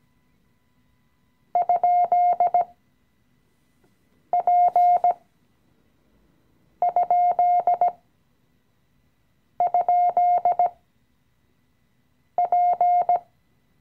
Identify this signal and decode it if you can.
Morse code, heard as a keyed tone, about 17 wpm, spaced as for 5 wpm, 710 Hz. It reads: ?P??P
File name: CW-Online-Kurs / Anfänger / Lektion 018